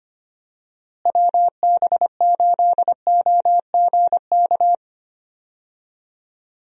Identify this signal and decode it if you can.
Morse code, heard as a keyed tone, about 25 wpm, 700 Hz. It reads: WB8OGK
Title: USA Call Signs - 6 Characters - Repeated 3x - 25wpm